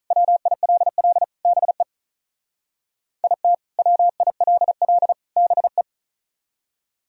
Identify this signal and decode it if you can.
Morse code, heard as a keyed tone, about 44 wpm, 700 Hz. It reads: WILL BE IT WILL BE